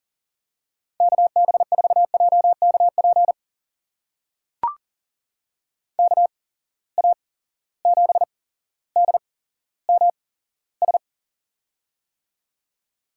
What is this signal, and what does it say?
Morse code, heard as a keyed tone, about 38 wpm, spaced as for 17 wpm, 700 Hz. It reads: KB4JKP KA7DMS
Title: USA Call Signs - 6 Characters - 8x Character Spacing - 40wpm